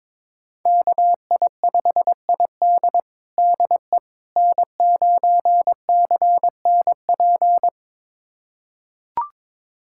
Morse code, heard as a keyed tone, about 21 wpm, 700 Hz. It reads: KI5ID DE N9CNP
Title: USA Callsign DE Callsign - 5 Characters - 22wpm